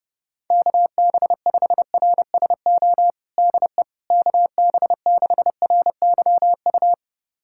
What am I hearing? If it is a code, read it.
Morse code, heard as a keyed tone, about 30 wpm, 700 Hz. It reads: KB5RSO DE KB6RYU
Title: USA Callsign DE Callsign - 6 Characters - 30wpm